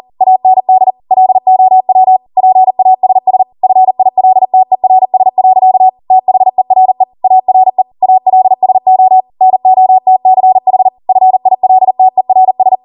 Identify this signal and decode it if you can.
Morse code, heard as a keyed tone, about 40 wpm, 750 Hz. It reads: AND LOW PASS FILTERS. THERE ARE ALSO NOTCH FILTERS